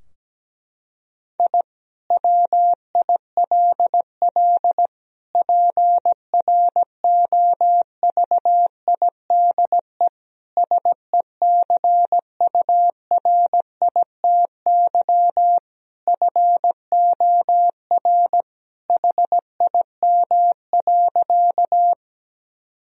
Morse code, heard as a keyed tone, about 17 wpm, 700 Hz. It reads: I WILL PROVIDE SECURITY FOR HIM.
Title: Sentences from Top 1000 Words - Review Part 4 - 17wpm